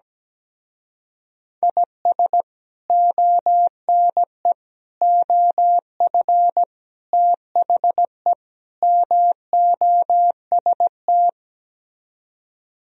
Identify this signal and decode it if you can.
Morse code, heard as a keyed tone, about 17 wpm, 700 Hz. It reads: IS ONE OF THE MOST
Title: Speed-Racing - Sets of 5 Words 17wpm